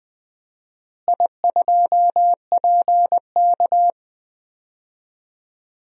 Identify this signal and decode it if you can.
Morse code, heard as a keyed tone, about 20 wpm, 700 Hz. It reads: I2PK
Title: DXCC-100 Call Signs - 4 Characters - Repeated 3x - 20wpm